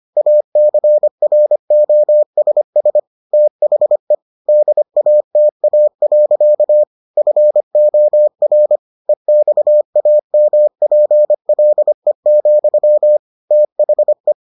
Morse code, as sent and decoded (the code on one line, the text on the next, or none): .- -.-. .-. --- ... ... / - .... / -.. .- - .- .-.-.- / ..-. --- .-. / . -..- .- -- .--. .-.. --..-- / - .... .
ACROSS THE DATA. FOR EXAMPLE, THE